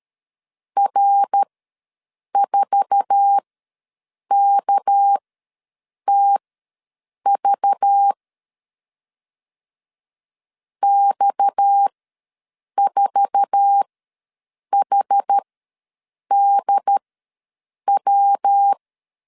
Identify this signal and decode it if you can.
Morse code, heard as a keyed tone, about 13 wpm, spaced as for 7 wpm, 800 Hz. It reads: R4KTV X4HDW
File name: LEZIONI / 10 SIGNS / 10A SLASH ws 12wpm 3443